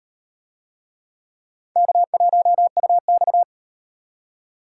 Morse code, heard as a keyed tone, about 36 wpm, 700 Hz. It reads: K1UX